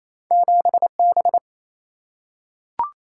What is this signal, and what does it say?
Morse code, heard as a keyed tone, about 28 wpm, 700 Hz. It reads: 7 B
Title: Single-Letter Single-Number 28wpm